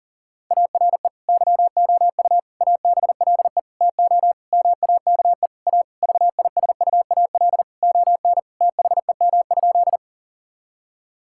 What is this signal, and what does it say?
Morse code, heard as a keyed tone, about 40 wpm, 700 Hz. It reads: ARE YOU ABLE TO MAKE A VISUAL ON THEM?
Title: Sentences from Top 1000 Words - Review Part 6 - 40wpm